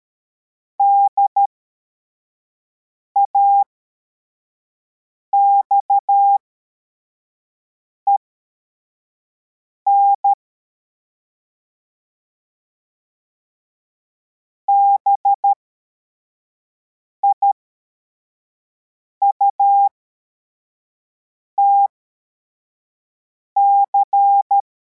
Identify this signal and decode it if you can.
Morse code, heard as a keyed tone, about 13 wpm, spaced as for 4 wpm, 800 Hz. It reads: DAXEN BIUTC
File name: formation / cw / Text41